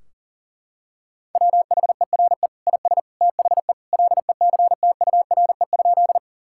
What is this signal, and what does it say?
Morse code, heard as a keyed tone, about 40 wpm, 700 Hz. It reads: WHERE IS THE LECTURE?
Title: Sentences from Top 1600 Words 40wpm